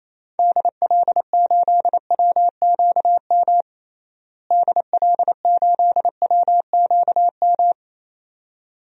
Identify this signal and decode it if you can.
Morse code, heard as a keyed tone, about 28 wpm, 700 Hz. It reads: DL8WQM DL8WQM
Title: DXCC-100 Call Signs - 6 Characters - Repeated 2x - 28wpm